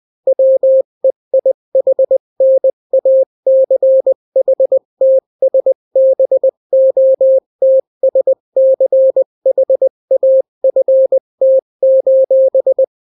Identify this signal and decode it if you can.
Morse code, heard as a keyed tone, about 20 wpm, 530 Hz. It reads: WEIHNACHTSBOTSCHAFT:
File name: daily / 18wpm